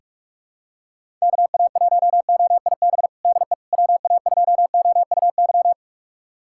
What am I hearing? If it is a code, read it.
Morse code, heard as a keyed tone, about 42 wpm, 690 Hz. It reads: KA1OIB DE WA2OUY
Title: USA Callsign DE Callsign 45wpm